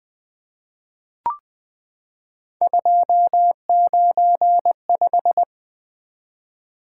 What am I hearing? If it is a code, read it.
Morse code, heard as a keyed tone, about 20 wpm, 700 Hz. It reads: 295
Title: Three Letter-Number Combinations - 20wpm